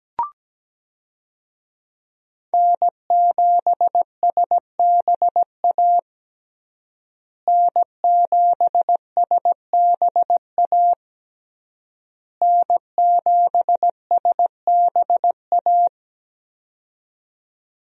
Morse code, heard as a keyed tone, about 17 wpm, 700 Hz. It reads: N7SBA N7SBA N7SBA
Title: USA Call Signs - 5 Characters - Repeated 3x - 17wpm